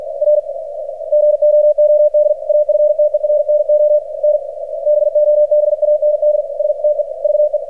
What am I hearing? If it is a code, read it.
Morse code, heard as a keyed tone, about 48 wpm, 600 Hz, with radio noise. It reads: R GOOD AFTERNON BOB AND STEVE
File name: archived / CRSnet